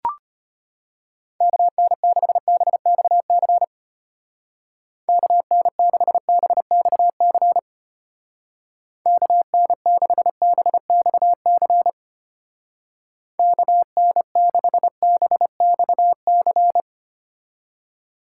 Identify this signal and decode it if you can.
Morse code, heard as a keyed tone, about 32 wpm, 700 Hz. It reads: KN6BXC KN6BXC KN6BXC KN6BXC